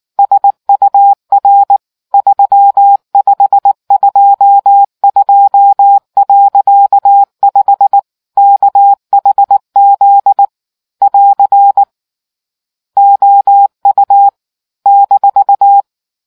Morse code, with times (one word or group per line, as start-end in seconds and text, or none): 0.2-1.8 SUR
2.1-8.0 3522.5
8.4-10.4 KHZ
11.0-11.8 <AR>
13.0-14.3 OU
14.9-15.8 -